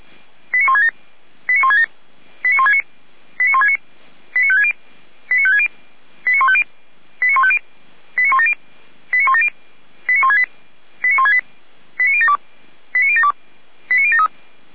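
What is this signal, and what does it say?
Five-tone selective call (ZVEI), heard as quick runs of five tones, tones of about 70 ms. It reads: 89167 89167 89169 89169 89560 89560 89150 89150 89170 89170 89168 89168 89072 89072 89073
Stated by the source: Five-tone ZVEI selective call received by SDR, from the feuerwehr-orlamuende.org fire brigade audio server.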